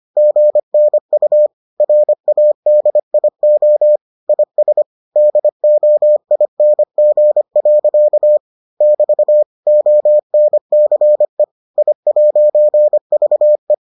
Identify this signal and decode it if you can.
Morse code, heard as a keyed tone, about 25 wpm, 600 Hz. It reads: GNU RADIO IS DOING. <BT> ONCE I'VE